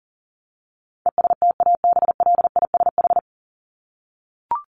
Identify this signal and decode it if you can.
Morse code, heard as a keyed tone, about 40 wpm, 700 Hz. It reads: ESTABLISH
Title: ED Suffix - Top 100 Words with Base Words 40wpm